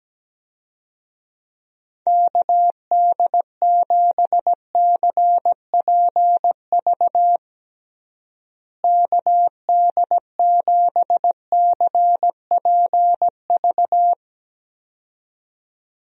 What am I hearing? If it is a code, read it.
Morse code, heard as a keyed tone, about 17 wpm, 700 Hz. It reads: KD7CPV KD7CPV
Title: USA Call Signs - 6 Characters - Repeated 2x - 17wpm